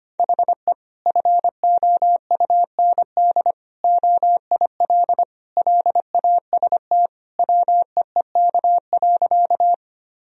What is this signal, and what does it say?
Morse code, heard as a keyed tone, about 25 wpm, 700 Hz. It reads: HE FOUND OIL LAST WEEK.
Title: Sentences from Top 1000 Words - Review Part 4 - 25wpm